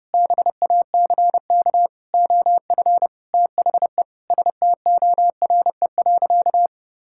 Morse code, heard as a keyed tone, about 30 wpm, 700 Hz. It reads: BACK OF THE STORE.
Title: Sentences from Top 1000 Words - Review Part 5 - 30wpm